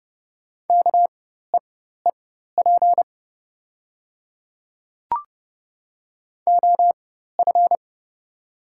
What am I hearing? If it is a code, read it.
Morse code, heard as a keyed tone, about 30 wpm, spaced as for 14 wpm, 700 Hz. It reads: KEEP OF
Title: 4x Character Spacing - Top 300 Words - 30wpm